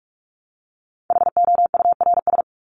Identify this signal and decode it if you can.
Morse code, heard as a keyed tone, about 44 wpm, 700 Hz. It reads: HOURS
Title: ES:S Suffix - Top 100 Words with Base Words 45wpm